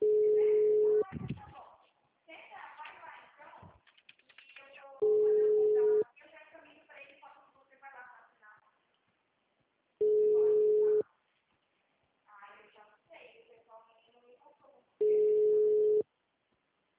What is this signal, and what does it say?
Ringback tone (Europe, 425 Hz), with speech.
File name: out- -2042- - - .6